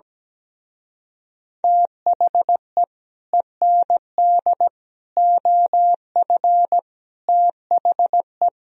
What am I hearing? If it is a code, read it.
Morse code, heard as a keyed tone, about 17 wpm, 700 Hz. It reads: THE END OF THE
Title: Speed-Racing - Sets of 5 Words 17wpm